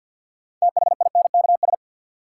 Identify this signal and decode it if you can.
Morse code, heard as a keyed tone, about 53 wpm, 700 Hz. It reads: THINKS
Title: Sentences from Top 1000 Words - Review Part 6 - New Word Drill - 50wpm